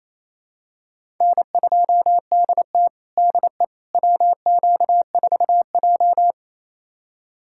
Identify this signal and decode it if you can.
Morse code, heard as a keyed tone, about 28 wpm, 700 Hz. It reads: N2DT DE WQ4J
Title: USA Callsign DE Callsign - 4 Characters - 28wpm